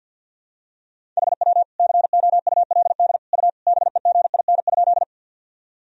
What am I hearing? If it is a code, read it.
Morse code, heard as a keyed tone, about 52 wpm, 700 Hz. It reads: HW COULD U BEGIN?
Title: Sentences from Top 300 Words and Abbreviations 50wpm